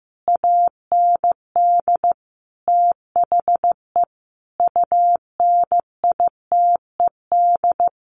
Morse code, heard as a keyed tone, about 14 wpm, 700 Hz. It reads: AND THE UNITED
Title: Sets of 5 Words Encore2 15wpm